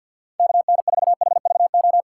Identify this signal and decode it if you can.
Morse code, heard as a keyed tone, about 53 wpm, 700 Hz. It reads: KN4HVO
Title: USA Call Signs - 4x Character Spacing - 50wpm